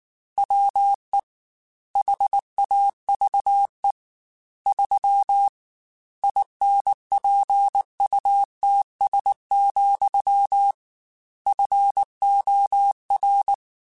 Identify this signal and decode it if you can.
Morse code, heard as a keyed tone, about 19 wpm, 800 Hz. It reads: WE HAVE 3 INPUTS, FOR
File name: Technical Subjects / Technical Subjects-mp3s / AND Gate @18